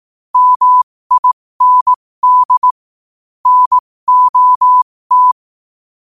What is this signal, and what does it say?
Morse code, heard as a keyed tone, about 18 wpm, spaced as for 15 wpm, 1000 Hz. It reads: MIND NOT